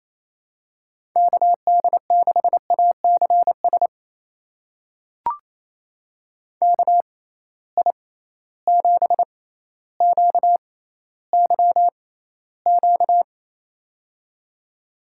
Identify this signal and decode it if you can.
Morse code, heard as a keyed tone, about 28 wpm, spaced as for 14 wpm, 700 Hz. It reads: KD6ACS KI7QYQ
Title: USA Call Signs - 6 Characters - 6x Character Spacing - 28wpm